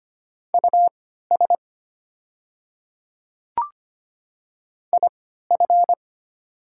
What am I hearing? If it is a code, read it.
Morse code, heard as a keyed tone, about 25 wpm, spaced as for 14 wpm, 690 Hz. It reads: US IF
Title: 3x Character Spacing - Top 200 Words - 25wpm